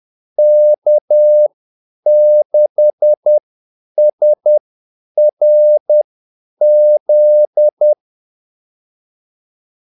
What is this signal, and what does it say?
Morse code, heard as a keyed tone, about 10 wpm, spaced as for 8 wpm, 600 Hz. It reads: K6SRZ
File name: Callsigns-10-8wpm